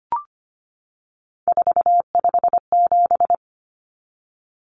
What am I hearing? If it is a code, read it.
Morse code, heard as a keyed tone, about 25 wpm, 700 Hz. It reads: 457
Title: Numbers - 3-Digit 25wpm